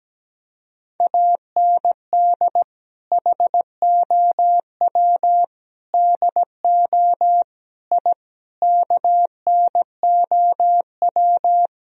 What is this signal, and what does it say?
Morse code, heard as a keyed tone, about 17 wpm, 700 Hz. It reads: AND HOW DO I KNOW
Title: Sentences from Top 1000 Words - Review Part 3 - 17wpm